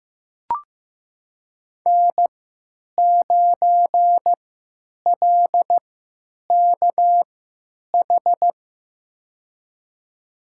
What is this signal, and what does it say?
Morse code, heard as a keyed tone, about 15 wpm, spaced as for 8 wpm, 700 Hz. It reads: N9LKH
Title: USA Call Signs - 5 Characters - 3x Character Spacing - 15wpm